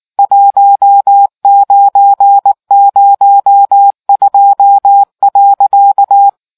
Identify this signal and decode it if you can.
Morse code, heard as a keyed tone, about 19 wpm, 800 Hz. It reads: 1902.